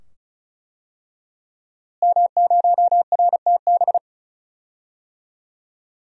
Morse code, heard as a keyed tone, about 36 wpm, 700 Hz. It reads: M0RTB
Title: DXCC-100 Call Signs - 5 Characters - Repeated 2x - 35wpm